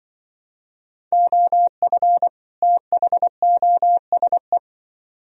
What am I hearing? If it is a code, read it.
Morse code, heard as a keyed tone, about 25 wpm, 700 Hz. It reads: OF THOSE
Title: Speed-Racing - Sets of 2 Words 20wpm